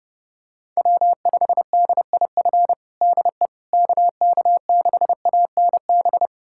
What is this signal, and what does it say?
Morse code, heard as a keyed tone, about 30 wpm, 700 Hz. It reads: W5DIF DE KK6ANB
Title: USA Callsign DE Callsign 30wpm